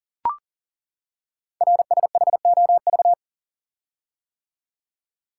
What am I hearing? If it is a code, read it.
Morse code, heard as a keyed tone, about 41 wpm, 690 Hz. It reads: RSHOV
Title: Code Groups - 5 Letter:Numbers - 40wpm